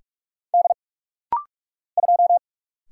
Morse code, heard as a keyed tone, about 44 wpm, 700 Hz. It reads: D 2
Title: Single Letter-Number Alt - Rapid-Fire 45wpm